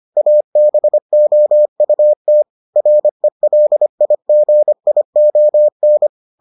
Morse code, heard as a keyed tone, about 25 wpm, 600 Hz. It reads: ABOUT RELIGION